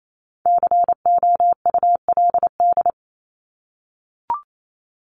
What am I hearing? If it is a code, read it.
Morse code, heard as a keyed tone, about 28 wpm, 700 Hz. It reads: COULD
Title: Top 200 Words Repeated 28wpm